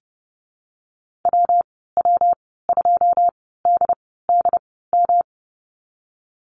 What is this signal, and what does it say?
Morse code, heard as a keyed tone, about 32 wpm, spaced as for 17 wpm, 700 Hz. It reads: WW2DDM
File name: Lesson 19g - Letter w - Call Signs 30wpm